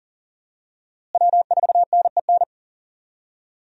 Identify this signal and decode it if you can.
Morse code, heard as a keyed tone, about 40 wpm, 700 Hz. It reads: W4NEN
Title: USA Call Signs - 5 Characters - Repeated 2x - 40wpm